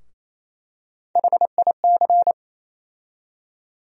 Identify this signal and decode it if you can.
Morse code, heard as a keyed tone, about 28 wpm, 700 Hz. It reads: HIC
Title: USA Call Sign Postfixes - 3 Letters 28wpm